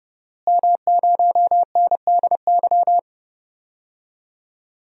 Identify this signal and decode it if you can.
Morse code, heard as a keyed tone, about 31 wpm, 700 Hz. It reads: M0NDY
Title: DXCC-100 Call Signs - 5 Characters - Repeated 3x - 30wpm